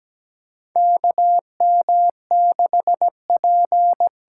Morse code, heard as a keyed tone, about 17 wpm, 700 Hz. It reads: KM6P
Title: USA Call Signs - 6 Characters - 8x Character Spacing - 17wpm